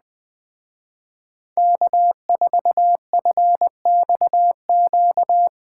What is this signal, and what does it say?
Morse code, heard as a keyed tone, about 20 wpm, 700 Hz. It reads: K4FXQ
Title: Speed Racing - USA 1x3 Call Signs 17wpm